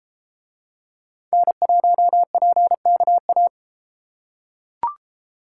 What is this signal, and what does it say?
Morse code, heard as a keyed tone, about 30 wpm, 700 Hz. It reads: N1PKA